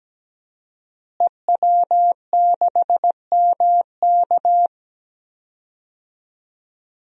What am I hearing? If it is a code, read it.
Morse code, heard as a keyed tone, about 17 wpm, 700 Hz. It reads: EW6MK
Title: DXCC-100 Call Signs - 5 Characters - Repeated 3x - 17wpm